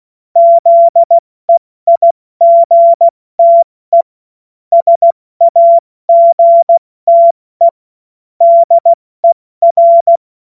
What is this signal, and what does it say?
Morse code, heard as a keyed tone, about 16 wpm, 680 Hz. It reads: ZEIGTE SAGTE DER
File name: Morsekurs / Witze / Witz 08. November 2022 70 F80